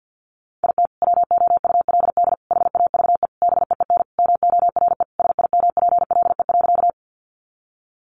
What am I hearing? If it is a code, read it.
Morse code, heard as a keyed tone, about 50 wpm, 700 Hz. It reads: IT WOULD HAVE BEEN MORE SIMPLE.
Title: Sentences from Top 400 Words 50wpm